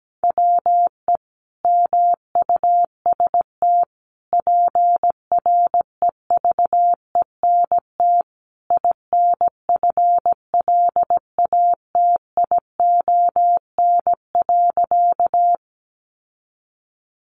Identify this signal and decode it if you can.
Morse code, heard as a keyed tone, about 17 wpm, 700 Hz. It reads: WE MUST PREVENT INFLATION.
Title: Sentences from Top 1100 Words 17wpm